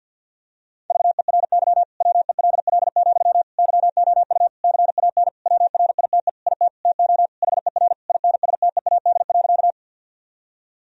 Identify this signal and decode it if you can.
Morse code, heard as a keyed tone, about 51 wpm, 700 Hz. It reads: VERY WELL, YOU CAN WRITE IT TO HER INSTEAD.
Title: Sentences from Top 1000 Words - Review Part 4 - 50wpm